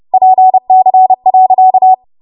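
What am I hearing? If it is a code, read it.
Morse code, heard as a keyed tone, about 30 wpm, 750 Hz. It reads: PC.
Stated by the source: Morse practice transmission at 30 WPM.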